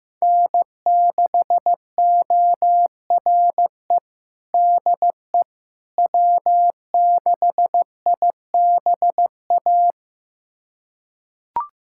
Morse code, heard as a keyed tone, about 15 wpm, 700 Hz. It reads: N6ORE DE W6IBA
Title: USA Callsign DE Callsign - 5 Characters - 15wpm